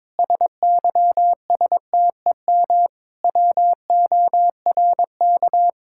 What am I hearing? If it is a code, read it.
Morse code, heard as a keyed tone, about 22 wpm, 700 Hz. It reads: SYSTEM WORK
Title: Sentences from Top 1000 Words - Review Part 5 - 22wpm